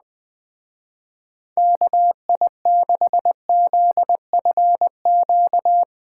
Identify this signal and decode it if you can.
Morse code, heard as a keyed tone, about 20 wpm, 700 Hz. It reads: KI6ZFQ
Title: Speed Racing - USA 2x3 Call Signs 17wpm